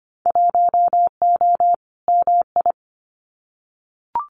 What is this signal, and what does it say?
Morse code, heard as a keyed tone, about 25 wpm, 700 Hz. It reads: 1O MI
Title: Winter Field Day Exchanges 25wpm